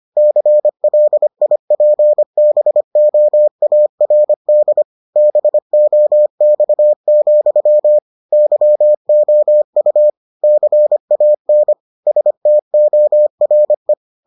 Morse code, heard as a keyed tone, about 25 wpm, 600 Hz. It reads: CLIPBOARD BOX, YOU CAN STORE